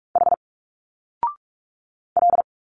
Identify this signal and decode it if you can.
Morse code, heard as a keyed tone, about 45 wpm, 700 Hz. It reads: H L
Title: Single Letters 45wpm